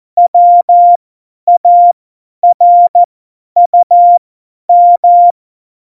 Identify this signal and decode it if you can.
Morse code, heard as a keyed tone, about 14 wpm, spaced as for 10 wpm, 700 Hz. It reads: WARUM
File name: Morsekurs / Witze / Witz 21. November 2017, 50 F70